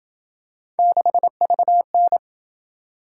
Morse code, heard as a keyed tone, about 27 wpm, 700 Hz. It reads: 6VN